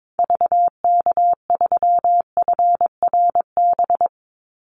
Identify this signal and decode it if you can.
Morse code, heard as a keyed tone, about 22 wpm, 700 Hz. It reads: VK3FRB
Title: USA Call Signs 22wpm